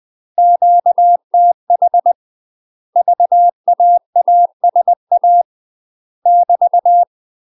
Morse code, heard as a keyed tone, about 20 wpm, 700 Hz. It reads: QTH VAASA <BT>
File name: UK MASTER QSOs TRAINING / 20WPM / UK MASTER QSOs 0068